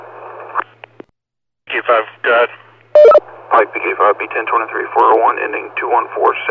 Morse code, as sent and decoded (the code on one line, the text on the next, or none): none